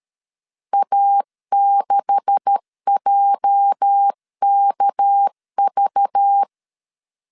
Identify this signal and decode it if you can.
Morse code, heard as a keyed tone, about 13 wpm, 800 Hz. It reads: A6JKV
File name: LEZIONI / 10 SIGNS / 10B INT 12wpm 2414